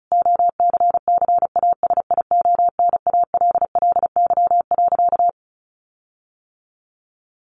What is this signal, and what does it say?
Morse code, heard as a keyed tone, about 34 wpm, 700 Hz. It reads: OCCASIONALLY.
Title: Sentences from Top 1300 Words 35wpm